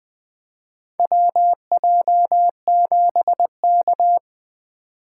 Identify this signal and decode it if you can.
Morse code, heard as a keyed tone, about 20 wpm, 700 Hz. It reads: WJ7K
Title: USA Call Signs - 2x Character Spacing - 20wpm